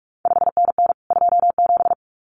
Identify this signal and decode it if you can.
Morse code, heard as a keyed tone, about 46 wpm, 700 Hz. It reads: 5NN 27
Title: CQ WW DX Contest Exchange Encore - 45wpm